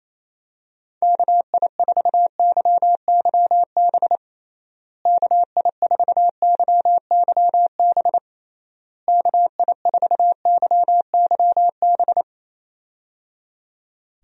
Morse code, heard as a keyed tone, about 27 wpm, 700 Hz. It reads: KI4YYB KI4YYB KI4YYB
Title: USA Call Signs - 6 Characters - Repeated 3x - 28wpm